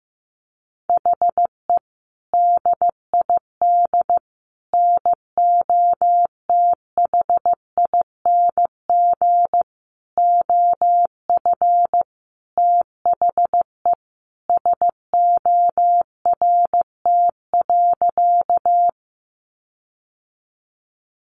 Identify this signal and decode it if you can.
Morse code, heard as a keyed tone, about 15 wpm, 700 Hz. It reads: HE DID NOTHING OF THE SORT.
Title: Sentences from Top 1300 Words 15wpm